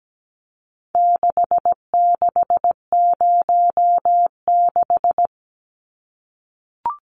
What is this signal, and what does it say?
Morse code, heard as a keyed tone, about 17 wpm, 700 Hz. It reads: 6606
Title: Numbers - 4-Digit 17wpm